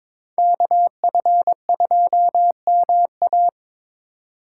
Morse code, heard as a keyed tone, about 26 wpm, 700 Hz. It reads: KF2MA
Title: Speed Racing - USA 2x2 Call Signs 22wpm